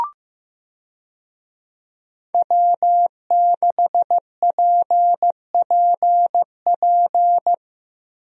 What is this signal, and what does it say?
Morse code, heard as a keyed tone, about 15 wpm, 700 Hz. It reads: W6PPP
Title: USA Call Signs - 5 Characters - Repeated 2x - 15wpm